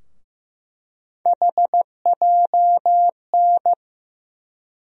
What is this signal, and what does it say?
Morse code, heard as a keyed tone, about 15 wpm, 700 Hz. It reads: HJN
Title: USA Call Sign Postfixes - 3 Letters 15wpm